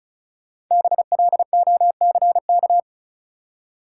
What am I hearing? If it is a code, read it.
Morse code, heard as a keyed tone, about 34 wpm, 700 Hz. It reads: BLOCK
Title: Five Letter Words - 35wpm